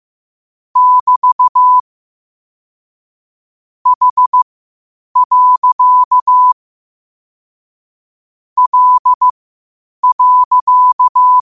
Morse code, heard as a keyed tone, about 15 wpm, spaced as for 8 wpm, 1000 Hz. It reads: <BT> H. L.
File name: qotdfiles / 8wpm